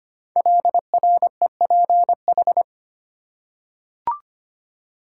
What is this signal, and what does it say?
Morse code, heard as a keyed tone, about 25 wpm, 700 Hz. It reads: LREPH
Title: Code Groups - 5 Letters - 25wpm